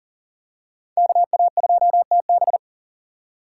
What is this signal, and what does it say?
Morse code, heard as a keyed tone, about 40 wpm, 700 Hz. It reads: KA2TB
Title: Code Groups - 5 Letter:Numbers - 40wpm